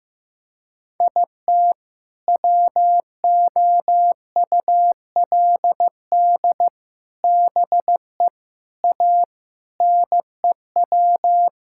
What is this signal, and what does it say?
Morse code, heard as a keyed tone, about 15 wpm, 700 Hz. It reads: IT WOULD BE A NEW
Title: Sentences from Top 1000 Words - Review Part 5 - 15wpm